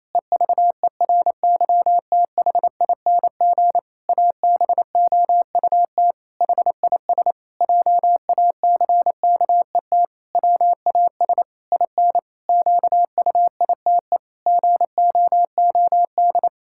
Morse code, heard as a keyed tone, about 28 wpm, 700 Hz. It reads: EVERYTHING ABOUT HIS JACKET WAS IN QUITE GOOD
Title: Pangram Challenge v2 - 28wpm